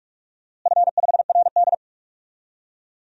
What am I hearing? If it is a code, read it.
Morse code, heard as a keyed tone, about 47 wpm, 700 Hz. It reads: F5RD
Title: DXCC-100 Call Signs - 4 Characters - Repeated 2x - 45wpm